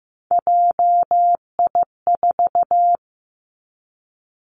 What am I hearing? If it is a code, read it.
Morse code, heard as a keyed tone, about 15 wpm, 700 Hz. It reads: JI4
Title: Three Challenging Letter-Number Combinations 15wpm